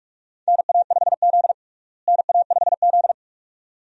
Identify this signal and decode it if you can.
Morse code, heard as a keyed tone, about 47 wpm, 700 Hz. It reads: NA5Z NA5Z
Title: USA Call Signs - 4 Characters - Repeated 2x - 45wpm